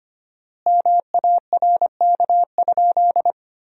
Morse code, heard as a keyed tone, about 25 wpm, 700 Hz. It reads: MARK?
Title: 4x Word-Spacing - Sentences from Top 300 Words - 25wpm